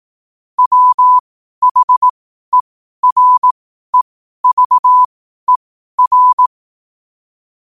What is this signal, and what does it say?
Morse code, heard as a keyed tone, about 18 wpm, spaced as for 12 wpm, 1000 Hz. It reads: WHEREVER
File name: qotdfiles / 12wpm